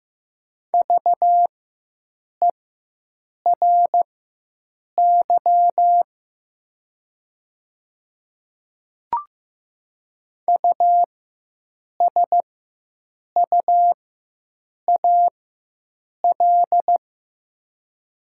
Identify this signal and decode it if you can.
Morse code, heard as a keyed tone, about 15 wpm, spaced as for 7 wpm, 700 Hz. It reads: VERY USUAL
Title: 4x Character Spacing - Top 500 Words - 15wpm